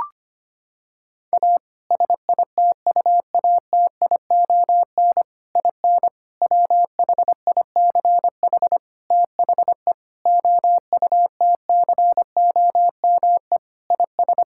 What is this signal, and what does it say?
Morse code, heard as a keyed tone, about 25 wpm, 700 Hz. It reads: A SITUATION IN WHICH THE OUTCOME IS
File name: practice / 25wpm / Sentences with 3-Word Phrases - 25wpm